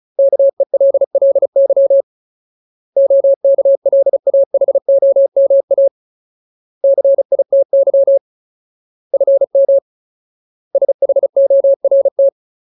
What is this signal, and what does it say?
Morse code, heard as a keyed tone, about 34 wpm, 550 Hz. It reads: KELLY OKLAHOMA CITY FM SHORT
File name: Ultimate Word List / 35WPM Ultimate